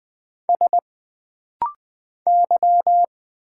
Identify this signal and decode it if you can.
Morse code, heard as a keyed tone, about 20 wpm, 700 Hz. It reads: S Y
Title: Single Letter-Number Alt - Rapid-Fire 20wpm